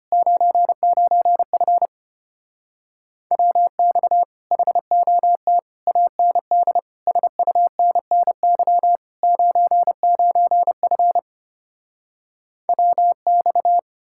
Morse code, heard as a keyed tone, about 33 wpm, 700 Hz. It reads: 99F WX HOT AND SUNNY 99F WX